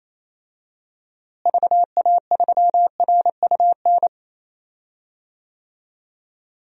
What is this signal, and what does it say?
Morse code, heard as a keyed tone, about 28 wpm, 700 Hz. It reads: VA3RUN
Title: DXCC-100 Call Signs - 6 Characters - Repeated 3x - 28wpm